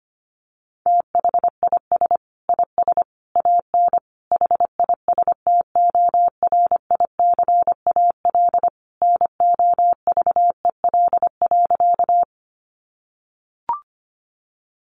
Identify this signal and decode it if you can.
Morse code, heard as a keyed tone, about 25 wpm, 700 Hz. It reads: THIS IS AN HISTORICAL NOVEL.
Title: Sentences from Top 1300 Words 25wpm